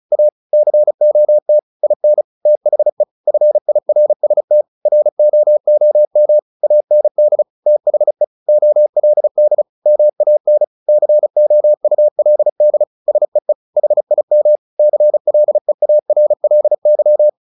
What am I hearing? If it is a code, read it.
Morse code, heard as a keyed tone, about 34 wpm, 600 Hz. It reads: A COT IN THE FIRST ROOM AND THE OLD MAN COULD SEE HIM CLEARLY
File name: the old man and the sea / the old man and the sea 1 35wpm